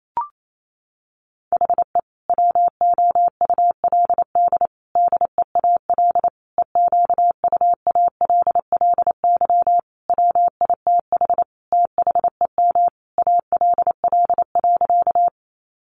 Morse code, heard as a keyed tone, about 28 wpm, 700 Hz. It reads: HE WOULD DEAL EQUALLY WITH THEM ALL.
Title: Sentences from Top 1400 Words 28wpm